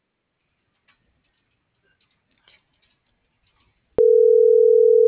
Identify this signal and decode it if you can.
Ringback tone (North America), 440 + 480 Hz.